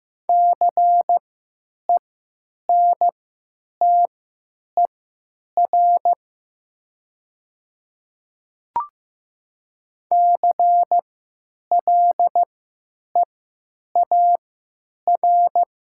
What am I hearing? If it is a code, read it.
Morse code, heard as a keyed tone, about 15 wpm, spaced as for 9 wpm, 700 Hz. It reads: CENTER CLEAR
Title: 3x Character Spacing - Top 500 Words - 15wpm